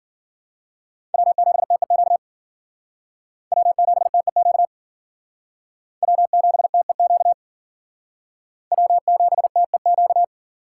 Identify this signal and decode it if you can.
Morse code, heard as a keyed tone, about 50 wpm, 700 Hz. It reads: W7TEQ W7TEQ W7TEQ W7TEQ